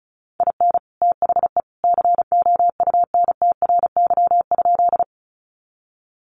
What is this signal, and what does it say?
Morse code, heard as a keyed tone, about 34 wpm, 700 Hz. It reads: IN THE COUNTRY?
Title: Sentences from Top 300 Words 35wpm